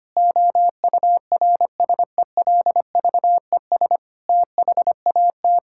Morse code, heard as a keyed tone, about 25 wpm, 700 Hz. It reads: OURSELVES THAT
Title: Sentences from Top 1600 Words 25wpm